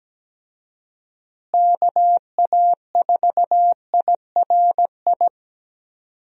Morse code, heard as a keyed tone, about 17 wpm, 700 Hz. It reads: KA4IRI
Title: USA Call Signs - 6 Characters - Repeated 2x - 17wpm